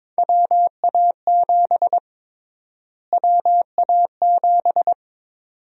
Morse code, heard as a keyed tone, about 22 wpm, 700 Hz. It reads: WA7 WA7
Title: USA Call Sign Prefixes - Repeated 2x - 22wpm